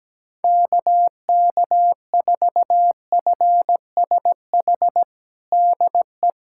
Morse code, heard as a keyed tone, about 17 wpm, 700 Hz. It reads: KK4FSH DE
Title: USA Callsign DE Callsign - 6 Characters - 17wpm